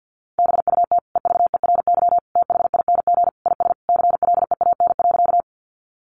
Morse code, heard as a keyed tone, about 50 wpm, 700 Hz. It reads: BUT EVERYTHING IS CLEAN.
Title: Sentences from Top 1100 Words 50wpm